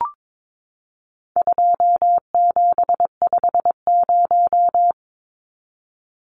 Morse code, heard as a keyed tone, about 22 wpm, 700 Hz. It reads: 2750
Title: Numbers - 4-Digit 22wpm